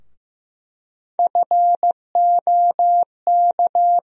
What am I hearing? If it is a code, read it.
Morse code, heard as a keyed tone, about 15 wpm, 700 Hz. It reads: FOK
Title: USA Call Sign Postfixes - 3 Letters 15wpm